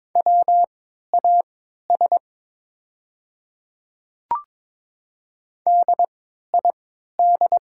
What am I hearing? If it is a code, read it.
Morse code, heard as a keyed tone, about 22 wpm, spaced as for 12 wpm, 690 Hz. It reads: WAS DID
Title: 3x Character Spacing - Top 100 Words - 22wpm